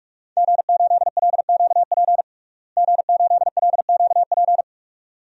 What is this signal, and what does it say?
Morse code, heard as a keyed tone, about 41 wpm, 700 Hz. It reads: G8LQP G8LQP
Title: DXCC-100 Call Signs - 5 Characters - Repeated 2x - 45wpm